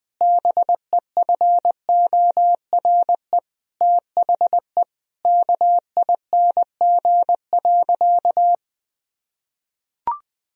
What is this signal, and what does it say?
Morse code, heard as a keyed tone, about 20 wpm, 700 Hz. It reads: BEFORE THE KING.
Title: Sentences from Top 1000 Words - Review Part 2 - 20wpm